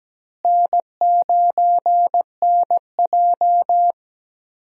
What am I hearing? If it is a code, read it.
Morse code, heard as a keyed tone, about 17 wpm, 700 Hz. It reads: N9NJ
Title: USA Call Signs - 4 Characters - Repeated 2x - 17wpm